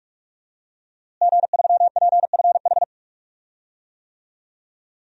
Morse code, heard as a keyed tone, about 46 wpm, 700 Hz. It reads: G3PFH
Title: DXCC-100 Call Signs - 5 Characters - Repeated 3x - 45wpm